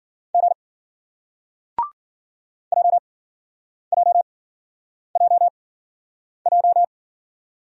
Morse code, heard as a keyed tone, about 54 wpm, spaced as for 9 wpm, 700 Hz. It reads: G JJJJ